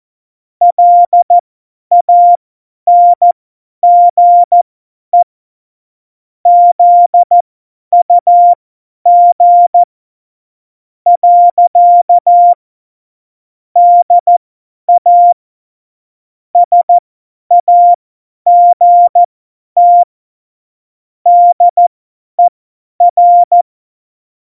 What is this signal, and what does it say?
Morse code, heard as a keyed tone, about 14 wpm, spaced as for 9 wpm, 700 Hz. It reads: LANGE ZUG. DA SAGT DER